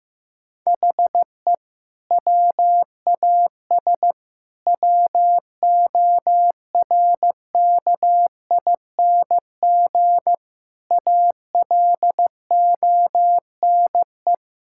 Morse code, heard as a keyed tone, about 15 wpm, 700 Hz. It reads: HE WAS WORKING ALONE
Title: Sentences from Top 1000 Words - Review Part 5 - 15wpm